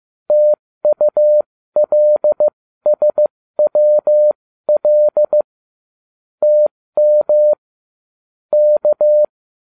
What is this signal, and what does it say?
Morse code, heard as a keyed tone, about 15 wpm, spaced as for 13 wpm, 600 Hz. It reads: TULSWL TM K
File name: cw / ubacw12 2